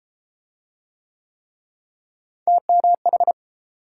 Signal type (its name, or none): Morse code